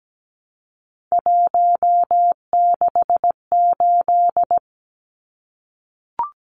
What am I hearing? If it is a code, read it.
Morse code, heard as a keyed tone, about 17 wpm, 700 Hz. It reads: 168
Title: Numbers - 3-Digit 17wpm